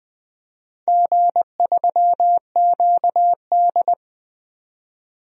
Morse code, heard as a keyed tone, about 20 wpm, 700 Hz. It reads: G3QD
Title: DXCC-100 Call Signs - 4 Characters - 20wpm